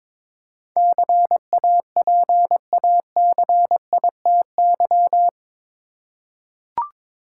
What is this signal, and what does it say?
Morse code, heard as a keyed tone, about 22 wpm, 700 Hz. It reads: CAPACITY
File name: practice / 22wpm / Eight Letter Words - 22wpm